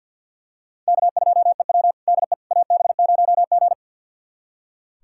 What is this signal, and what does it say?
Morse code, heard as a keyed tone, about 46 wpm, 700 Hz. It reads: K2EW DE AB0G
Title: USA Callsign DE Callsign - 4 Characters - 50wpm